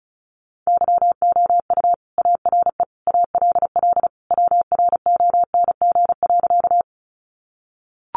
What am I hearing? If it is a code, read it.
Morse code, heard as a keyed tone, about 34 wpm, 700 Hz. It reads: YOU ARE ALL WRONG.
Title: Sentences from Top 900 Words 35wpm